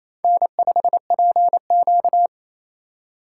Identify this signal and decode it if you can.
Morse code, heard as a keyed tone, about 28 wpm, 700 Hz. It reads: N5PQ
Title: USA Call Signs - 4 Characters - 6x Character Spacing - 28wpm